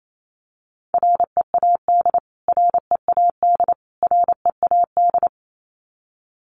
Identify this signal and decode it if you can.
Morse code, heard as a keyed tone, about 28 wpm, 700 Hz. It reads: READ READ READ
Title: Top 500 Words Repeated 28wpm